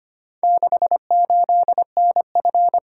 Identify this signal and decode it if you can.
Morse code, heard as a keyed tone, about 25 wpm, 700 Hz. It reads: 68NF
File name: practice / 25wpm / Four Letter-Number Combinations - 25wpm